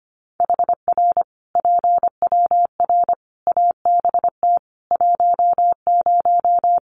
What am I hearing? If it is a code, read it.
Morse code, heard as a keyed tone, about 25 wpm, 700 Hz. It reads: HR PWR ABT 10
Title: Common QSO Phrases 25wpm